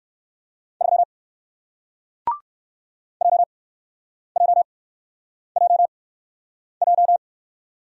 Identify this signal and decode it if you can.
Morse code, heard as a keyed tone, about 60 wpm, spaced as for 9 wpm, 700 Hz. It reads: 3 JJJJ